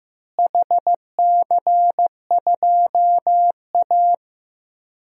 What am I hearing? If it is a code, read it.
Morse code, heard as a keyed tone, about 15 wpm, 700 Hz. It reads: HC2A
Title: DXCC-100 Call Signs - 4 Characters - Repeated 3x - 15wpm